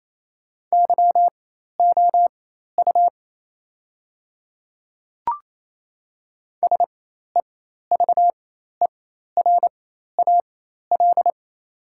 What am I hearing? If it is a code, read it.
Morse code, heard as a keyed tone, about 28 wpm, spaced as for 13 wpm, 690 Hz. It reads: YOU SEVERAL
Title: 4x Character Spacing - Top 500 Words - 28wpm